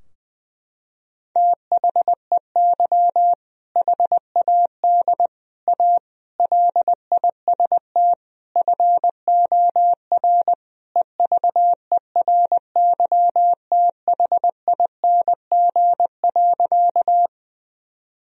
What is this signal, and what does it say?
Morse code, heard as a keyed tone, about 20 wpm, 700 Hz. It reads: THEY HAD A LIST FOR EVERYTHING.
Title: Sentences from Top 1000 Words - Review Part 2 - 20wpm